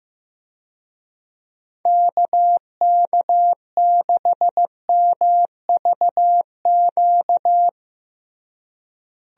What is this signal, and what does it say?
Morse code, heard as a keyed tone, about 15 wpm, 700 Hz. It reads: KK6MVQ
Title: USA Call Signs - 6 Characters - Repeated 3x - 15wpm